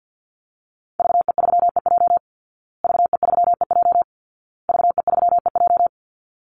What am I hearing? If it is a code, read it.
Morse code, heard as a keyed tone, about 50 wpm, 700 Hz. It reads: VE3EJ VE3EJ VE3EJ
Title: Call Signs Repeated 2x - 50wpm